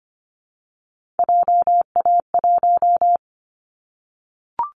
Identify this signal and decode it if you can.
Morse code, heard as a keyed tone, about 25 wpm, 700 Hz. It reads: JA1
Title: Callsign Prefixes for Top 25 DXCC - 25wpm